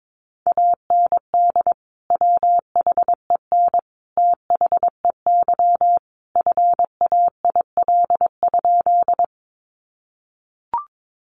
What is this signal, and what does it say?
Morse code, heard as a keyed tone, about 22 wpm, 700 Hz. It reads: AND WHEN THEY FAIL?
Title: Sentences from Top 1100 Words 22wpm